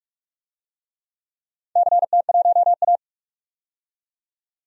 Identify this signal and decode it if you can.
Morse code, heard as a keyed tone, about 44 wpm, 690 Hz. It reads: CT1A
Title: DXCC-100 Call Signs - 4 Characters - 45wpm